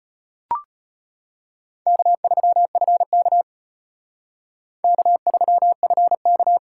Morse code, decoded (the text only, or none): K3FK K3FK